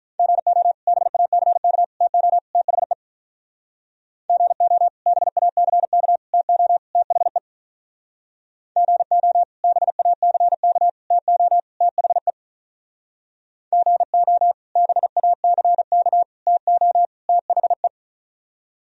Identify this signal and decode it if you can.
Morse code, heard as a keyed tone, about 46 wpm, 700 Hz. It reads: GO BACK TO THE GO BACK TO THE GO BACK TO THE GO BACK TO THE